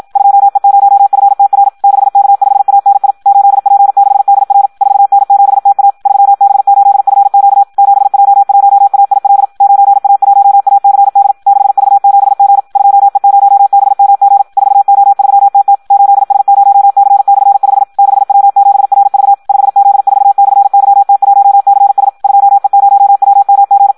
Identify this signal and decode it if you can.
Morse code, heard as a keyed tone, about 54 wpm, 800 Hz. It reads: 2E0FTR 6K5ANI 8P6DR 4N7TA 3B8FQ 7J1AIL 9A1ACD 6V7G 2E0BMG 4O3TT 8S0C/5 6W7RV 5Z4/YT1CS 2E0WMG
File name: Audio practica CW / qrq-tx-54 2